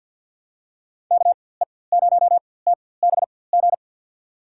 Morse code, heard as a keyed tone, about 51 wpm, spaced as for 23 wpm, 700 Hz. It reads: KE0TBG
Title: USA Call Signs - 6 Characters - 4x Character Spacing - 50wpm